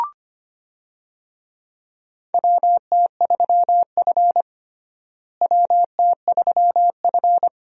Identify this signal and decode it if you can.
Morse code, heard as a keyed tone, about 25 wpm, 700 Hz. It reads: WT3F WT3F
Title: USA Call Signs - 4 Characters - Repeated 3x - 25wpm